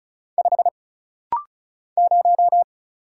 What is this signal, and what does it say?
Morse code, heard as a keyed tone, about 36 wpm, 700 Hz. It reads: 5 0
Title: Single Letter-Number Alt - Rapid-Fire 35wpm